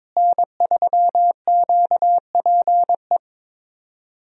Morse code, heard as a keyed tone, about 22 wpm, 700 Hz. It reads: N3QPE
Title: USA Call Signs - 5 Characters - Repeated 2x - 22wpm